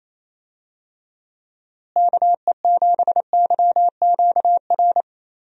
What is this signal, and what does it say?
Morse code, heard as a keyed tone, about 28 wpm, 700 Hz. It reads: KE7YQR
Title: USA Call Signs - 6 Characters - Repeated 2x - 28wpm